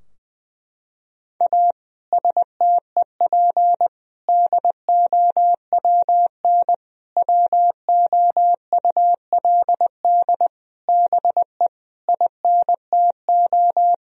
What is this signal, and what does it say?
Morse code, heard as a keyed tone, about 20 wpm, 700 Hz. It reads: A STEP DOWN WOULD BE INTO
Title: Sentences from Top 1000 Words - Review Part 4 - 20wpm